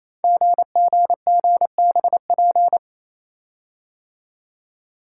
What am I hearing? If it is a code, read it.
Morse code, heard as a keyed tone, about 29 wpm, 700 Hz. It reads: GGGBP
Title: Code Groups - 5 Letters - 28wpm